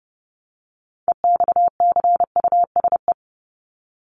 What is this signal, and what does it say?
Morse code, heard as a keyed tone, about 30 wpm, 700 Hz. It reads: EXCUSE
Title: Top 501-750 Words 30wpm